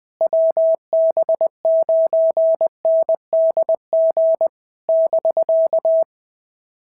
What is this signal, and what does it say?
Morse code, heard as a keyed tone, about 20 wpm, 650 Hz. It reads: WB9NDG <BK>
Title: POTA QSO Practice - 20wpm